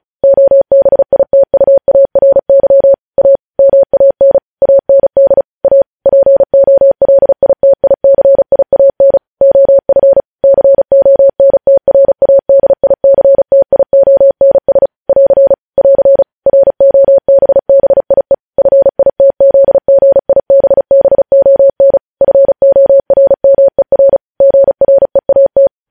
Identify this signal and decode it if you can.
Morse code, heard as a keyed tone, about 36 wpm, 550 Hz. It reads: OBITUARY A MAN AND A POLITICIAN OF CONTRADICTIONS <AR> <AR> ROBBIE FITZGIBBON FORMER GREAT